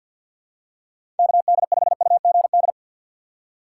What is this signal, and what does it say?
Morse code, heard as a keyed tone, about 52 wpm, 700 Hz. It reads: KD5UGD
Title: USA Call Signs - 6 Characters - 6x Character Spacing - 50wpm